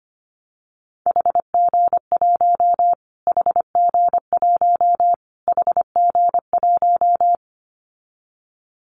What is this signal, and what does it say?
Morse code, heard as a keyed tone, about 25 wpm, 700 Hz. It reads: HG1 HG1 HG1
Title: Callsign Prefixes for Top 25 DXCC - Repeated 3x - 25wpm